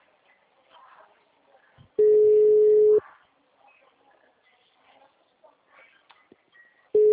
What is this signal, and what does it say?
Ringback tone (Europe, 425 Hz).
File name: out- -2014- - - .229